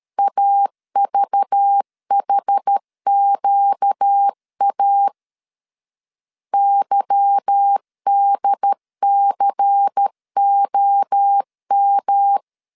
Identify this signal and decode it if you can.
Morse code, heard as a keyed tone, about 13 wpm, 800 Hz. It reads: AVHQA YDCOM
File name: CW / DET / 1 9 1500C S A 12wpm 3800